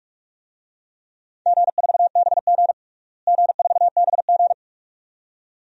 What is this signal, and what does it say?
Morse code, heard as a keyed tone, about 41 wpm, 700 Hz. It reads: G4BG G4BG
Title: DXCC-100 Call Signs - 4 Characters - Repeated 2x - 45wpm